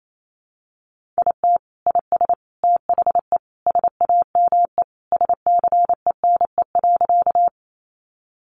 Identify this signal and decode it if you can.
Morse code, heard as a keyed tone, about 28 wpm, 700 Hz. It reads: IT IS THE SAME SCENE.
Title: Sentences from Top 1100 Words 28wpm